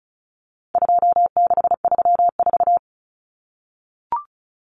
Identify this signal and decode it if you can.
Morse code, heard as a keyed tone, about 34 wpm, 700 Hz. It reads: 2634
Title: Numbers - 4-Digit 35wpm